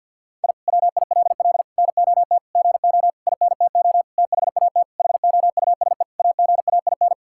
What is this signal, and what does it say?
Morse code, heard as a keyed tone, about 52 wpm, 700 Hz. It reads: I WILL NOT GO INTO THAT HOUSE AGAIN
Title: Sentences from Top 1000 Words - Review Part 6 - 50wpm